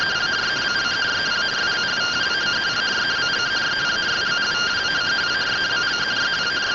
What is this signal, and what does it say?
Two-tone teleprinter signal, about 170 Hz shift.
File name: findings / AXIAL fr130 3mm 409.gen-08 rtty rx